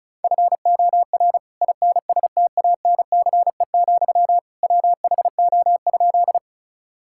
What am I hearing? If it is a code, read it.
Morse code, heard as a keyed tone, about 34 wpm, 700 Hz. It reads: FOR INSTANCE, WHO?
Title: Sentences from Top 1000 Words - Review Part 5 - 35wpm